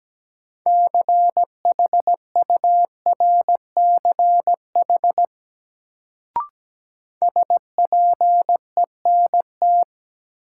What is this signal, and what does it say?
Morse code, heard as a keyed tone, about 17 wpm, 700 Hz. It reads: CHURCH SPENT
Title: Sets of 5 Words - New Word Drill - 17wpm